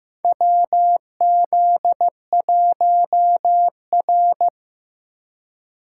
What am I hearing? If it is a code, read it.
Morse code, heard as a keyed tone, about 15 wpm, 700 Hz. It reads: WZ1R
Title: USA Call Signs - 4 Characters - 4x Character Spacing - 15wpm